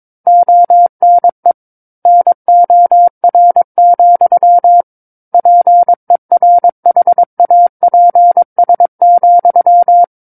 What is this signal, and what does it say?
Morse code, heard as a keyed tone, about 22 wpm, 700 Hz. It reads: ONE NOR, PERHAPS,
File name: Book One / 13-Chapter 13 - 22 WPM